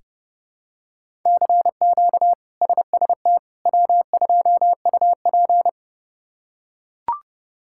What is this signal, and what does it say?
Morse code, heard as a keyed tone, about 30 wpm, 700 Hz. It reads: CQ SST W2UP
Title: K1USN SST QSO Practice - 30wpm